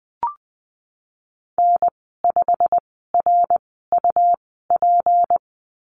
Morse code, heard as a keyed tone, about 20 wpm, 700 Hz. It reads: N 5 R U P
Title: 154 Lesson 25g - Letter p - Call Signs 20wpm